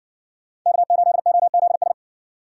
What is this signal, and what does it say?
Morse code, heard as a keyed tone, about 54 wpm, 700 Hz. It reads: K8OZS